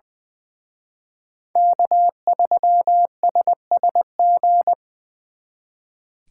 Morse code, heard as a keyed tone, about 23 wpm, 700 Hz. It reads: K3SSG
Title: Speed Racing - USA 1x3 Call Signs 20wpm